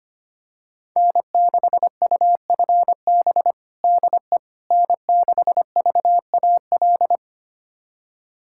Morse code, heard as a keyed tone, about 25 wpm, 700 Hz. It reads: N6UFB DE N6VAL
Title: USA Callsign DE Callsign - 5 Characters - 25wpm